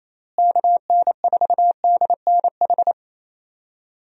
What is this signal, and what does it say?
Morse code, heard as a keyed tone, about 28 wpm, 700 Hz. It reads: KN4DNH
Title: USA Call Signs - 6 Characters - Repeated 3x - 28wpm